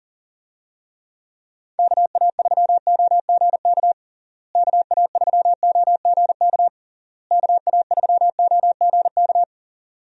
Morse code, heard as a keyed tone, about 37 wpm, 700 Hz. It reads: KA3OGK KA3OGK KA3OGK
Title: USA Call Signs - 6 Characters - Repeated 3x - 40wpm